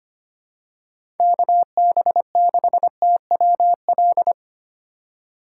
Morse code, heard as a keyed tone, about 25 wpm, 700 Hz. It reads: KB6TWL
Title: USA Call Signs - 6 Characters - 3x Character Spacing - 25wpm